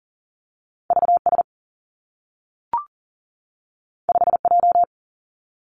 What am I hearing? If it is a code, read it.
Morse code, heard as a keyed tone, about 40 wpm, 700 Hz. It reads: V S 5 J
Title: Two Challenging Letter-Number Combinations 40wpm